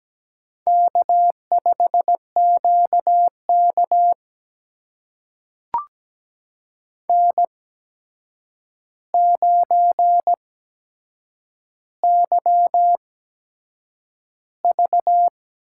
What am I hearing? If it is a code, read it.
Morse code, heard as a keyed tone, about 17 wpm, spaced as for 7 wpm, 700 Hz. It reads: K5QK N9YV
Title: USA Call Signs - 8x Character Spacing - 17wpm